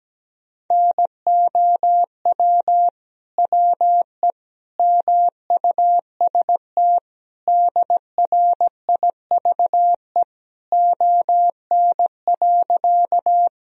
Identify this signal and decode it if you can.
Morse code, heard as a keyed tone, about 17 wpm, 700 Hz. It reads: NOW WE MUST DRIVE ON.
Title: Sentences from Top 1000 Words - Review Part 2 - 17wpm